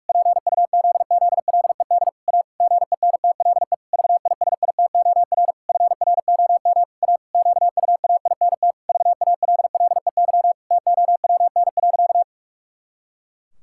Morse code, heard as a keyed tone, about 46 wpm, 700 Hz. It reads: PUZZLED A GENTLE VISITOR FROM A QUAINT VALLEY TOWN.
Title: Pangram Challenge v2 - 45wpm